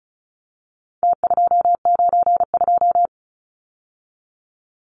Morse code, heard as a keyed tone, about 34 wpm, 700 Hz. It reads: T292
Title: Numbers - Contest Style - 4-Digit 35wpm